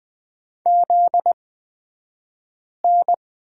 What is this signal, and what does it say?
Morse code, heard as a keyed tone, about 20 wpm, spaced as for 5 wpm, 700 Hz. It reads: ZN
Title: 249 Lesson 40d - BK - Two Characters 20wpm